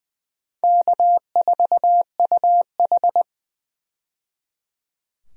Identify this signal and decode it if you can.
Morse code, heard as a keyed tone, about 20 wpm, 700 Hz. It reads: K4UH
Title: USA Call Signs - 4 Characters - 20wpm